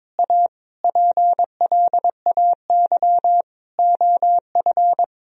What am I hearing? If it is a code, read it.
Morse code, heard as a keyed tone, about 22 wpm, 700 Hz. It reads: A PLAY OF
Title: Sentences from Top 1000 Words - Review Part 4 - 22wpm